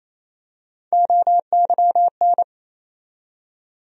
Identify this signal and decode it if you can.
Morse code, heard as a keyed tone, about 28 wpm, 700 Hz. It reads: OYN
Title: USA Call Sign Postfixes - 3 Letters 28wpm